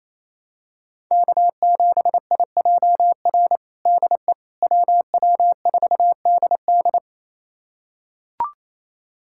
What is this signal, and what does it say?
Morse code, heard as a keyed tone, about 28 wpm, 700 Hz. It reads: K7IJR DE WW4DD
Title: USA Callsign DE Callsign - 5 Characters - 28wpm